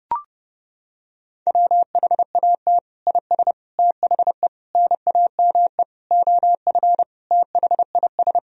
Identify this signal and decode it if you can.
Morse code, heard as a keyed tone, about 32 wpm, 700 Hz. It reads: WHAT IS THE NAME OF THIS
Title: Sentences from Top 1000 Words - Review Part 1 - 30wpm